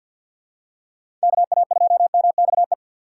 Morse code, heard as a keyed tone, about 47 wpm, 700 Hz. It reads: KA2MXE
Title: USA Call Signs - 6 Characters - 6x Character Spacing - 50wpm